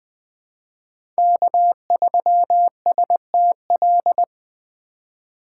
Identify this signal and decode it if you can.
Morse code, heard as a keyed tone, about 21 wpm, 700 Hz. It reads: K3STL
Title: 2023 POTA Hunter Call Signs - 20wpm